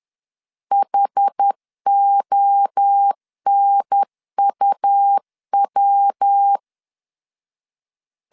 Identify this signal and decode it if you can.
Morse code, heard as a keyed tone, about 10 wpm, 800 Hz. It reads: HONUW
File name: CW / DET / 1 3 1500C S C 10wpm 3335